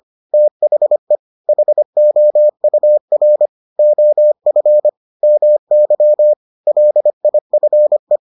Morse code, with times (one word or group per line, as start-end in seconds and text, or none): 0.3-1.2 THE
1.5-3.5 HOUR
3.8-4.9 OF
5.2-6.3 MY
6.7-8.2 LIFE